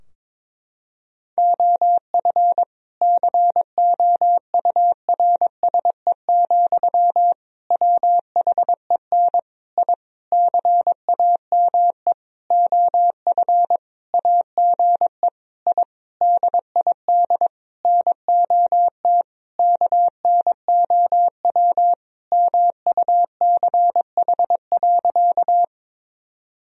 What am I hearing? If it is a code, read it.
Morse code, heard as a keyed tone, about 22 wpm, 700 Hz. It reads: OF COURSE, WHEN I CAME OF AGE I DID NOT KNOW MUCH.
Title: Sentences from Top 1000 Words - Review Part 3 - 22wpm